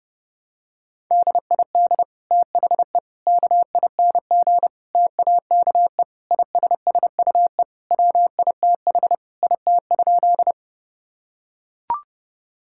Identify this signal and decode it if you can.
Morse code, heard as a keyed tone, about 32 wpm, 700 Hz. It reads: DID THE KING TAKE ISSUE WITH IT?
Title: Sentences from Top 1000 Words - Review Part 6 - 30wpm